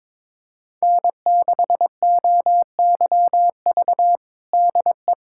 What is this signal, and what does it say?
Morse code, heard as a keyed tone, about 22 wpm, 700 Hz. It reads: N6OYV DE
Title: USA Callsign DE Callsign - 5 Characters - 22wpm